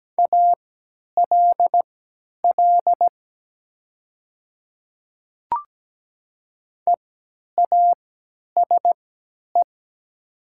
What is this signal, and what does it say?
Morse code, heard as a keyed tone, about 17 wpm, spaced as for 10 wpm, 700 Hz. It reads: ALL EASE
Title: 3x Character Spacing - Top 300 Words - 17wpm